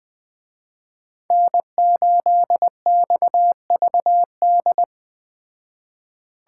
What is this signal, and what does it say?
Morse code, heard as a keyed tone, about 20 wpm, 700 Hz. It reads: N8XVD
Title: Speed Racing - USA 1x3 Call Signs 17wpm